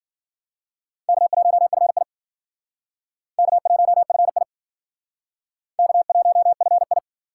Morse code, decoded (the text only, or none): K1FI K1FI K1FI